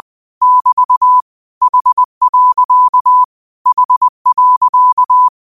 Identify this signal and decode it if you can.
Morse code, heard as a keyed tone, about 20 wpm, 1000 Hz. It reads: <BT> H. H.